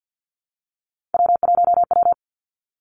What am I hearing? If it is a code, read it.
Morse code, heard as a keyed tone, about 50 wpm, 700 Hz. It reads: W1W
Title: Three Challenging Letter-Number Combinations 50wpm